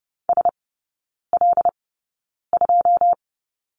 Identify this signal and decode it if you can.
Morse code, heard as a keyed tone, about 30 wpm, spaced as for 9 wpm, 700 Hz. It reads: SL2
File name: 081 Lesson 14e - Number 2 - Three Characters 30wpm